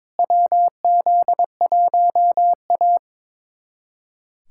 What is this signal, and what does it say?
Morse code, heard as a keyed tone, about 22 wpm, 700 Hz. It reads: WZ1A
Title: USA Callsign DE Callsign - 4 Characters - 22wpm